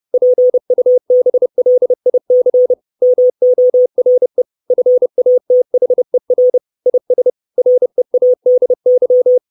Morse code, heard as a keyed tone, about 30 wpm, 500 Hz. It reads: PUBLIC MORE FATHER IS READY
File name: 632 Words 30WPM